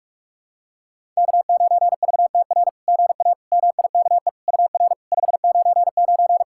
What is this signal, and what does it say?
Morse code, heard as a keyed tone, about 44 wpm, 690 Hz. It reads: K9VTR GA MIKE UR 599
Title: SOTA QSO Practice 45wpm